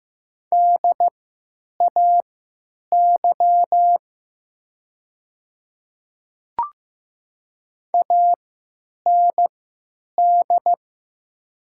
Morse code, heard as a keyed tone, about 15 wpm, spaced as for 9 wpm, 700 Hz. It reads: DAY AND
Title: 3x Character Spacing - Top 100 Words - 15wpm